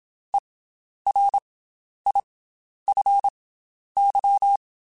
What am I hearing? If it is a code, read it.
Morse code, heard as a keyed tone, about 27 wpm, spaced as for 11 wpm, 800 Hz. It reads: ERIFY
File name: Special INGs / Special INGs-mp3s / Slow ing @25